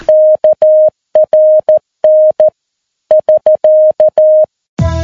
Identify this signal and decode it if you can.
Morse code, heard as a keyed tone, about 14 wpm, 620 Hz. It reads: KRN <SK>
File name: PI4AA Uitzending